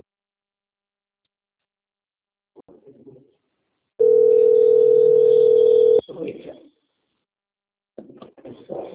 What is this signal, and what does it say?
Ringback tone (North America), 440 + 480 Hz, with speech.
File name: out- - - - - .10505